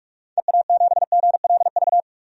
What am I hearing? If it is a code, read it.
Morse code, heard as a keyed tone, about 42 wpm, 700 Hz. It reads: EA7GLV
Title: DXCC-100 Call Signs - 6 Characters - 45wpm